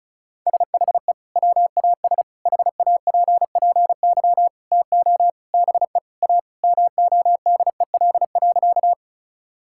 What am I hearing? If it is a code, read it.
Morse code, heard as a keyed tone, about 36 wpm, 700 Hz. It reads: SHE WAS HAPPY TO BE A MODEL.
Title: Sentences from Top 1000 Words - Review Part 5 - 35wpm